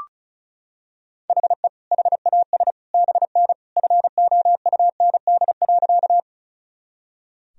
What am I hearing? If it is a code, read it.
Morse code, heard as a keyed tone, about 36 wpm, 700 Hz. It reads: HE HAS BN FOUND.
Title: Sentences from Top 300 Words and Abbreviations 35wpm